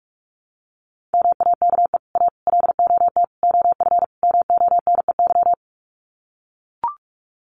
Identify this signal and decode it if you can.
Morse code, heard as a keyed tone, about 44 wpm, 700 Hz. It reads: MAKE A LOT OF MONEY
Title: Sets of 5 Words Encore2 45wpm